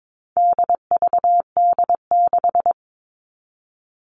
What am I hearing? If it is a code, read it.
Morse code, heard as a keyed tone, about 22 wpm, 700 Hz. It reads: DVD6
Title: All Cut Numbers - 4-Digit 22wpm